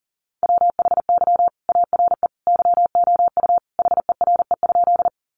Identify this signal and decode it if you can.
Morse code, heard as a keyed tone, about 40 wpm, 700 Hz. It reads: WHY ARE YOU HERE?
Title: Sentences from Top 200 Words 40wpm